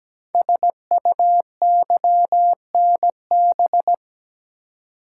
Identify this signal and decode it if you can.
Morse code, heard as a keyed tone, about 17 wpm, 700 Hz. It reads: SUYNB
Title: Code Groups - 5 Letters - 17wpm